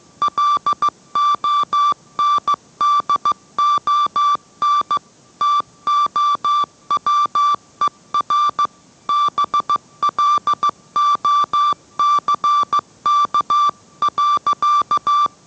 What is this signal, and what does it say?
Morse code, heard as a keyed tone, about 15 wpm, 1170 Hz. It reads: LONDON TOWER BLOCK.